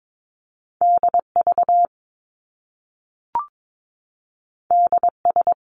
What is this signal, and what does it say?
Morse code, heard as a keyed tone, about 22 wpm, 700 Hz. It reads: DV DS
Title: Top Two Letter Combinations in Words 22wpm